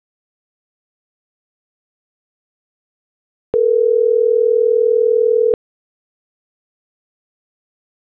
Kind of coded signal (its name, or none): telephone call progress tone